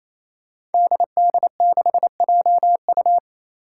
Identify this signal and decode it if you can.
Morse code, heard as a keyed tone, about 28 wpm, 700 Hz. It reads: DD6JU
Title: Five Challenging Letter-Number Combinations 28wpm